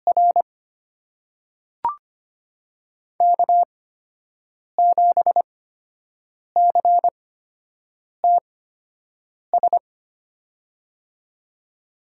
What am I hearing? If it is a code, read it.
Morse code, heard as a keyed tone, about 25 wpm, spaced as for 7 wpm, 700 Hz. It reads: R K7CTS
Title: USA Call Signs - 5 Characters - 8x Character Spacing - 25wpm